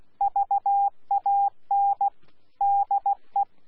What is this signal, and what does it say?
Morse code, heard as a keyed tone, about 16 wpm, 800 Hz. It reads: VAN DE
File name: Z PI4VRZ / PI4VRZ 14-jun-2025